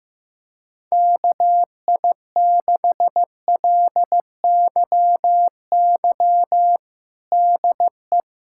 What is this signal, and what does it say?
Morse code, heard as a keyed tone, about 15 wpm, 700 Hz. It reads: KI6LYY DE
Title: USA Callsign DE Callsign - 6 Characters - 15wpm